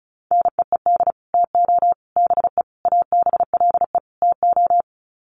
Morse code, heard as a keyed tone, about 34 wpm, 700 Hz. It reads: NEED TO BE ABLE TO
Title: Sets of 5 Words Encore 35wpm